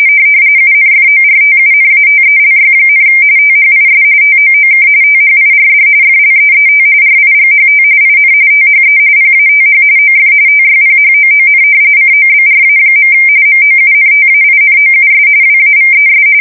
RTTY, about 45.45 baud. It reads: UNFORTUNATELY, THIS TOOK THIRD FLEET RIGHT IN THE PATH OF TYPHOON COBRA. MOUNTAINOUS WAVES
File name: fsk170 / battleshipnj